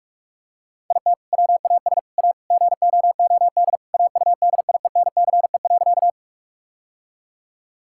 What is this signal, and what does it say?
Morse code, heard as a keyed tone, about 46 wpm, 700 Hz. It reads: IT WAS A GOOD AUDIENCE.
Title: Sentences from Top 1000 Words - Review Part 5 - 45wpm